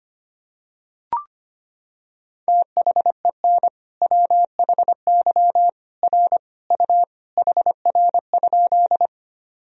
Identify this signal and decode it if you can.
Morse code, heard as a keyed tone, about 25 wpm, 700 Hz. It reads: THEN WHY R U HR?
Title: Sentences from Top 200 Words and Abbreviations 25wpm